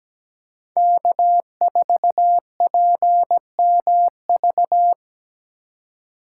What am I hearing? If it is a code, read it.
Morse code, heard as a keyed tone, about 20 wpm, 700 Hz. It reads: K4PMV
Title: Speed Racing - USA 1x3 Call Signs 17wpm